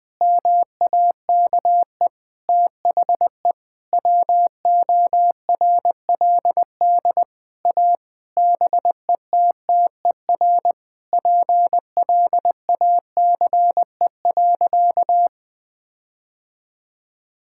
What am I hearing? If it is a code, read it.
Morse code, heard as a keyed tone, about 21 wpm, 700 Hz. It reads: MAKE THE WORLD A BETTER PLACE.
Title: Sentences with 7-Word Phrases - 20wpm